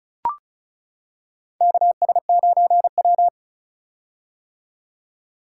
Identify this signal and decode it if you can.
Morse code, heard as a keyed tone, about 34 wpm, 700 Hz. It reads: KS9W
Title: K1USN SST QSO Practice - 35wpm